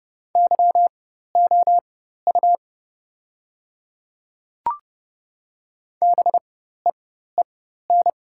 Morse code, heard as a keyed tone, about 30 wpm, spaced as for 14 wpm, 700 Hz. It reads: YOU BEEN
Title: 4x Character Spacing - Top 200 Words - 30wpm